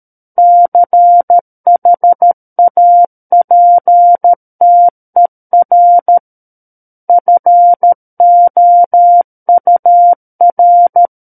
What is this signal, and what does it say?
Morse code, heard as a keyed tone, about 13 wpm, 700 Hz. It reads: CHAPTER FOUR
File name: Book One / 04-Chapter 4 - 13 WPM